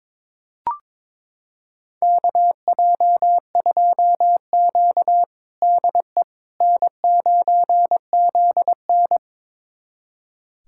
Morse code, heard as a keyed tone, about 22 wpm, 700 Hz. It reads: KJ2Q DE N9ZN
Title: USA Callsign DE Callsign - 4 Characters - 22wpm